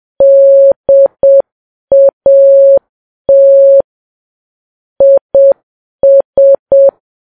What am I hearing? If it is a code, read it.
Morse code, heard as a keyed tone, about 7 wpm, 550 Hz. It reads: DAT IS
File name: Files Web tls / Morse on9fv / morse7kps